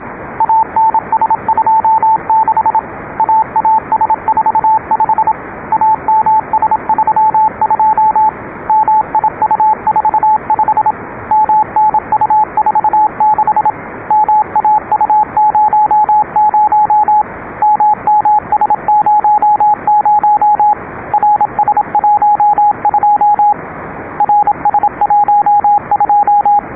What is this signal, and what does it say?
Morse code, heard as a keyed tone, about 26 wpm, 880 Hz, with radio noise. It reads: ANS26 AAS45 AMS32 MIU45 MNU46 MAU00 MMS00 RS12 RS12